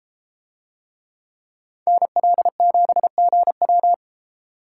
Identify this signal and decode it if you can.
Morse code, heard as a keyed tone, about 32 wpm, 700 Hz. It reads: NL7GW